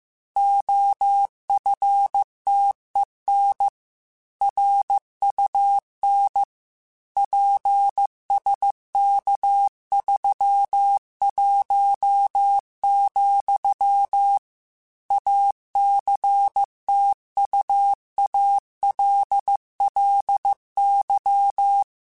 Morse code, heard as a keyed tone, about 15 wpm, 800 Hz. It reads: OFTEN RUN PSK31, ACTUALLY
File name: QSOs / QSOs-mp3s / kc7sac@13